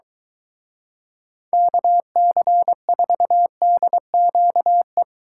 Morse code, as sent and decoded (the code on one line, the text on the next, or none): -.- -.-. ....- -.. --.- .
KC4DQE